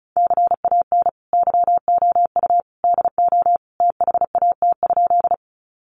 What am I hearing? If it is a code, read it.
Morse code, heard as a keyed tone, about 34 wpm, 700 Hz. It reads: CAN YOU DO THAT?
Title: Sentences from Top 100 Words 35wpm